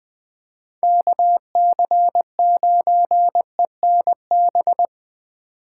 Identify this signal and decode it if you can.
Morse code, heard as a keyed tone, about 21 wpm, 700 Hz. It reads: KC9ENB